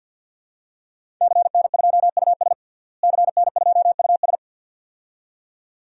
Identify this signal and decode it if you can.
Morse code, heard as a keyed tone, about 53 wpm, 700 Hz. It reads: CN2US CN2US
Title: DXCC-100 Call Signs - 5 Characters - Repeated 2x - 50wpm